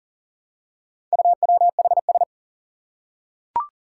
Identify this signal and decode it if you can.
Morse code, heard as a keyed tone, about 37 wpm, 700 Hz. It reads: UWHS
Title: Four Challenging Letter-Number Combinations 40wpm